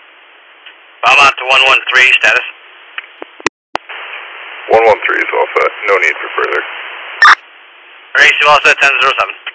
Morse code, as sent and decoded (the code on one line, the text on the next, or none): none